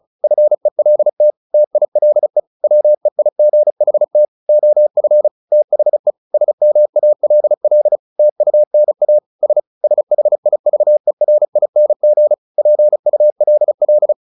Morse code, heard as a keyed tone, about 34 wpm, 600 Hz. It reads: FELT TILE WEIGHT OF THE SMALL TUNA S SHIVERING PULL
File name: the old man and the sea / the old man and the sea 1 35wpm